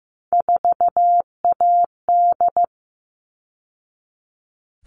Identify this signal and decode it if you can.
Morse code, heard as a keyed tone, about 15 wpm, 700 Hz. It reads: 4AD
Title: All Cut Numbers - 3-Digit 15wpm